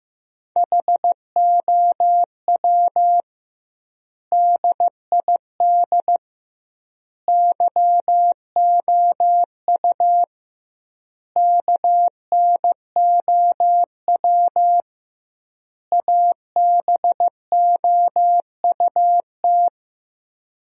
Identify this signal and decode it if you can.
Morse code, heard as a keyed tone, about 15 wpm, 700 Hz. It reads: HOW DID YOU KNOW ABOUT
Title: 2x Word-Spacing - Sentences from Top 100 Words - 15wpm